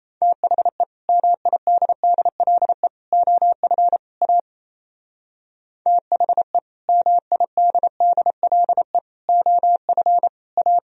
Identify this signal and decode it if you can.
Morse code, heard as a keyed tone, about 34 wpm, 700 Hz. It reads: THE MIDDLE OF A THE MIDDLE OF A
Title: Speed-Racing - Sets of 4 Words 28wpm